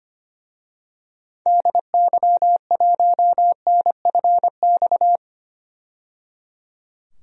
Morse code, heard as a keyed tone, about 25 wpm, 700 Hz. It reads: DY1NFX
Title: DXCC-100 Call Signs - 6 Characters - 25wpm